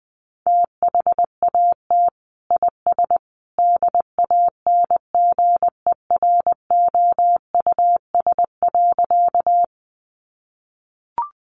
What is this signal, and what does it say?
Morse code, heard as a keyed tone, about 19 wpm, 700 Hz. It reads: THAT IS DANGEROUS.
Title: Sentences from Top 1200 Words 20wpm